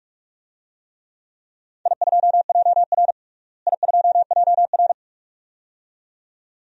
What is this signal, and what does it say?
Morse code, heard as a keyed tone, about 46 wpm, 700 Hz. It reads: I2JR I2JR
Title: DXCC-100 Call Signs - 4 Characters - Repeated 2x - 45wpm